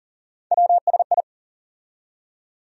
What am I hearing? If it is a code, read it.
Morse code, heard as a keyed tone, about 40 wpm, 700 Hz. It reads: WSI
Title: USA Call Sign Postfixes - 3 Letters 40wpm